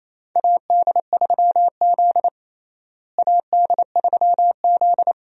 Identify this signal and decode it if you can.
Morse code, heard as a keyed tone, about 28 wpm, 700 Hz. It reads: AD3Z AD3Z
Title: USA Call Signs - 4 Characters - Repeated 2x - 28wpm